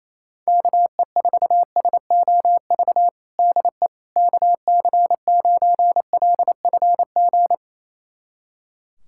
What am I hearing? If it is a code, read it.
Morse code, heard as a keyed tone, about 28 wpm, 700 Hz. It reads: KE4SOV DE KC9LFG
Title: USA Callsign DE Callsign - 6 Characters - 28wpm